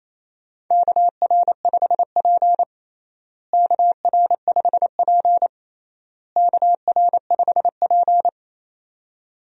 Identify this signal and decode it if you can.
Morse code, heard as a keyed tone, about 28 wpm, 700 Hz. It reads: KR5P KR5P KR5P
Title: USA Call Signs - 4 Characters - Repeated 3x - 28wpm